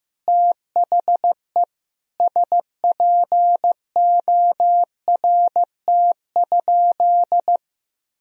Morse code, heard as a keyed tone, about 15 wpm, 700 Hz. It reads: THE SPORT?
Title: Sentences from Top 1000 Words - Review Part 4 - 15wpm